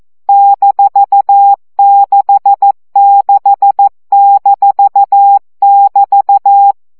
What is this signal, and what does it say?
Morse code, heard as a keyed tone, about 14 wpm, 800 Hz. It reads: -66-<BT>
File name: CW gyakorlas b 6 egyenlo minusz 12 baud